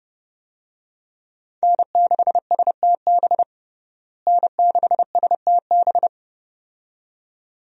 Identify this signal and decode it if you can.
Morse code, heard as a keyed tone, about 30 wpm, 700 Hz. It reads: N6STB N6STB
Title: USA Call Signs - 5 Characters - Repeated 2x - 30wpm